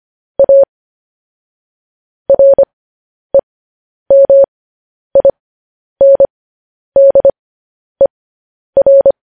Morse code, heard as a keyed tone, about 25 wpm, spaced as for 10 wpm, 550 Hz. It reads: A REMINDER
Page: news-headlines-in-morse-code-at-10-wpm